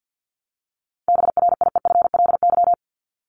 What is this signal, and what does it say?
Morse code, heard as a keyed tone, about 50 wpm, 700 Hz. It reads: BRIEFLY
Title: LY Suffix - Top 100 Words with Base Words 50wpm